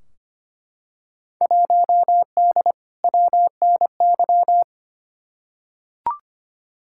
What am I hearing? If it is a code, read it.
Morse code, heard as a keyed tone, about 25 wpm, 700 Hz. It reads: 1D WNY
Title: Field Day Exchanges - 25wpm